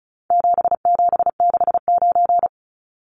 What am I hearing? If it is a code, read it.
Morse code, heard as a keyed tone, about 34 wpm, 700 Hz. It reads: 7769
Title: Numbers - 4-Digit 35wpm